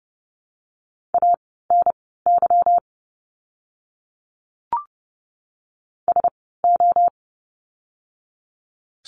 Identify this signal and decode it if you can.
Morse code, heard as a keyed tone, about 30 wpm, spaced as for 17 wpm, 700 Hz. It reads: ANY SO
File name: course / 30:10wpm / 230 Lesson 37f - Letter x - Words 30wpm